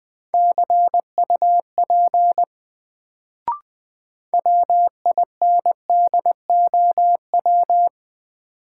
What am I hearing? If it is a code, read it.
Morse code, heard as a keyed tone, about 20 wpm, 700 Hz. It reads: CUP WINDOW
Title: Sentences from Top 1600 Words - New Word Drill - 20wpm